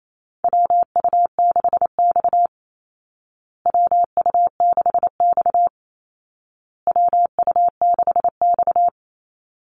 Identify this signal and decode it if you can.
Morse code, heard as a keyed tone, about 28 wpm, 700 Hz. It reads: WU6X WU6X WU6X
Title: Call Signs Repeated 2x - 28wpm